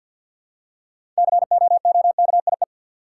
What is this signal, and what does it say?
Morse code, heard as a keyed tone, about 50 wpm, 700 Hz. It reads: COOKIE
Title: Six Letter Words - 50wpm